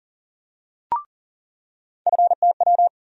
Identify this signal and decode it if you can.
Morse code, heard as a keyed tone, about 40 wpm, 700 Hz. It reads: FTW
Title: USA Call Sign Postfixes - 3 Letters 40wpm